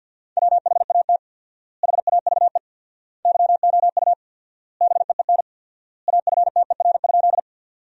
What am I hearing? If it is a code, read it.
Morse code, heard as a keyed tone, about 50 wpm, 700 Hz. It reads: WHAT HAVE YOU BEEN AFTER?
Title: 4x Word-Spacing - Sentences from Top 200 Words - 50wpm